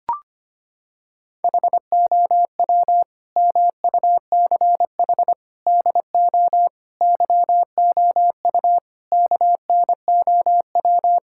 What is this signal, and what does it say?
Morse code, heard as a keyed tone, about 25 wpm, 700 Hz. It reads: HOW MUCH DO YOU KNOW
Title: Sentences from Top 1000 Words - Review Part 6 - 25wpm